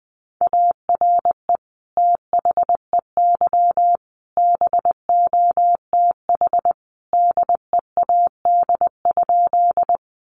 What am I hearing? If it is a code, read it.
Morse code, heard as a keyed tone, about 20 wpm, 700 Hz. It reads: ARE THEY BOTH DEAD?
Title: Sentences from Top 1500 Words 20wpm